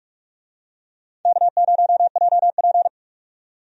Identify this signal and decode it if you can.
Morse code, heard as a keyed tone, about 46 wpm, 700 Hz. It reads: K0JP
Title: USA Call Signs - 4 Characters - 45wpm